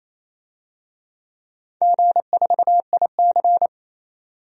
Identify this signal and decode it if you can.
Morse code, heard as a keyed tone, about 27 wpm, 700 Hz. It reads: G4IC
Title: DXCC-100 Call Signs - 4 Characters - 28wpm